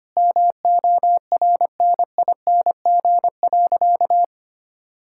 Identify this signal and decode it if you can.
Morse code, heard as a keyed tone, about 25 wpm, 700 Hz. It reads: MORNING.
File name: practice / 25wpm / Difficult Sentences with Hints - 25wpm